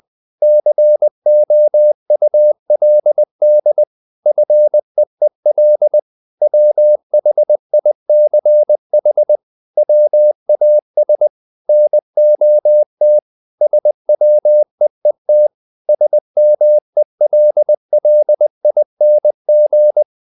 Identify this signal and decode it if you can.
Morse code, heard as a keyed tone, about 20 wpm, 600 Hz. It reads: COULD FEEL WHICH WAS NOT SWEET SMELLING
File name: the old man and the sea / the old man and the sea part 1 20wpm